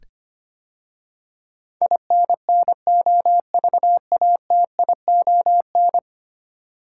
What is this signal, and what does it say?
Morse code, heard as a keyed tone, about 25 wpm, 700 Hz. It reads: INNOVATION
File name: 10 Letter Words - First 3 Letters Given - 25wpm